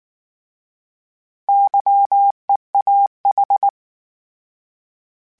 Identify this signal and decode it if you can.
Morse code, heard as a keyed tone, about 19 wpm, 800 Hz. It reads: YEAH